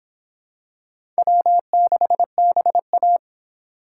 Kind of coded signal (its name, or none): Morse code